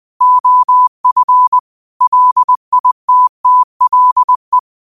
Morse code, heard as a keyed tone, about 20 wpm, 1000 Hz. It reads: OF LITTLE
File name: qotdfiles / 20wpm